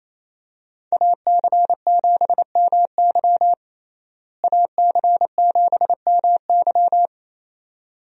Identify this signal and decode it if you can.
Morse code, heard as a keyed tone, about 28 wpm, 700 Hz. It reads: AC7MY AC7MY
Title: USA Call Signs - 5 Characters - Repeated 2x - 28wpm